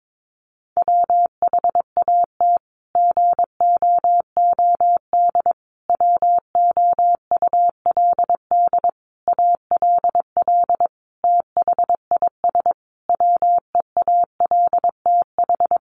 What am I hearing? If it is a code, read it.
Morse code, heard as a keyed tone, about 22 wpm, 700 Hz. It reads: WHAT GOOD WOULD ALL THIS WEALTH
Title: Sentences from Top 1200 Words 22wpm